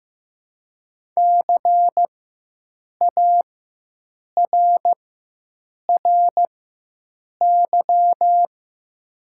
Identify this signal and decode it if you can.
Morse code, heard as a keyed tone, about 15 wpm, spaced as for 7 wpm, 700 Hz. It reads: CARRY
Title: 4x Character Spacing - Top 300 Words - 15wpm